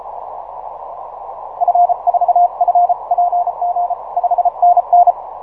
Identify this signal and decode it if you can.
Morse code, heard as a keyed tone, about 34 wpm, 710 Hz, with radio noise. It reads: F4FPG 5NN
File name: simple-file-list / 5A1AL 6MCW